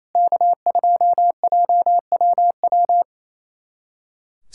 Morse code, heard as a keyed tone, about 28 wpm, 700 Hz. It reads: K2JWW
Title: SOTA QSO Practice 28wpm